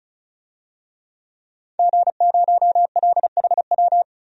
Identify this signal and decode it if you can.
Morse code, heard as a keyed tone, about 36 wpm, 700 Hz. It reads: G0LHW
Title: DXCC-100 Call Signs - 5 Characters - 35wpm